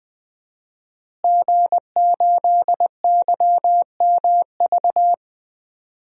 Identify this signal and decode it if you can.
Morse code, heard as a keyed tone, about 20 wpm, 700 Hz. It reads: G8YMV
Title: DXCC-100 Call Signs - 5 Characters - Repeated 2x - 20wpm